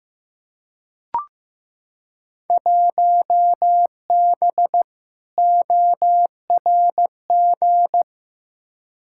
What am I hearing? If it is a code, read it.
Morse code, heard as a keyed tone, about 15 wpm, 700 Hz. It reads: 1B ORG
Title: Field Day Exchanges Encore - 15wpm